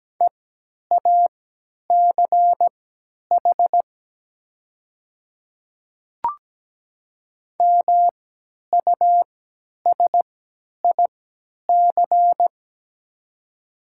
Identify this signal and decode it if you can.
Morse code, heard as a keyed tone, about 17 wpm, spaced as for 10 wpm, 700 Hz. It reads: EACH MUSIC
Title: 3x Character Spacing - Top 300 Words - 17wpm